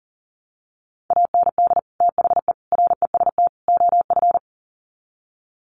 Morse code, heard as a keyed tone, about 40 wpm, 700 Hz. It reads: AND THE REST OF
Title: Sets of 4 Words 40wpm